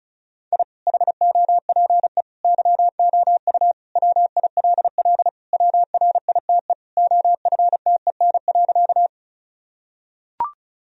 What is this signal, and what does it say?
Morse code, heard as a keyed tone, about 36 wpm, 700 Hz. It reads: I HOPE YOU WILL WRITE OFTEN.
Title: Sentences from Top 1000 Words - Review Part 1 - 35wpm